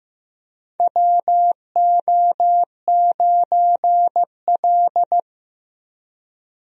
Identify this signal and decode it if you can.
Morse code, heard as a keyed tone, about 15 wpm, 700 Hz. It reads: WO9L
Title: USA Call Signs - 4 Characters - 15wpm